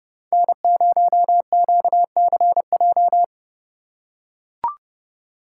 Morse code, heard as a keyed tone, about 30 wpm, 700 Hz. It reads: N0QCJ
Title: USA Call Signs - 6x Character Spacing - 30wpm